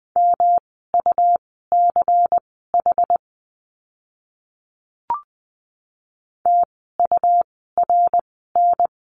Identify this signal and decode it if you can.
Morse code, heard as a keyed tone, about 20 wpm, spaced as for 15 wpm, 700 Hz. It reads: MUCH TURN
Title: 146 Lesson 24f - Letter y - Words 20wpm